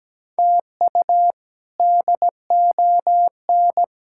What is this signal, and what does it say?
Morse code, heard as a keyed tone, about 17 wpm, 700 Hz. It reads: TU DON
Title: K1USN SST QSO Practice - 17wpm